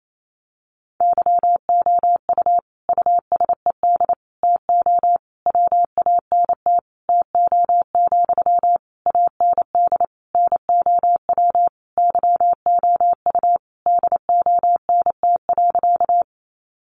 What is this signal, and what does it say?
Morse code, heard as a keyed tone, about 28 wpm, 700 Hz. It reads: YOU USED TO WANT TO, AND NOW YOU DONT.
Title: Sentences from Top 600 Words 28wpm